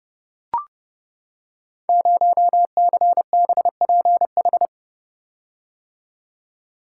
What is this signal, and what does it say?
Morse code, heard as a keyed tone, about 30 wpm, 700 Hz. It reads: 0CBPH
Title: Code Groups - 5 Letter:Numbers - 30wpm